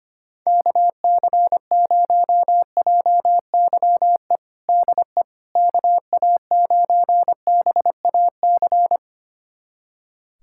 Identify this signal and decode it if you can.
Morse code, heard as a keyed tone, about 25 wpm, 700 Hz. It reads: KC0JYE DE KA9BAC
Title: USA Callsign DE Callsign - 6 Characters - 25wpm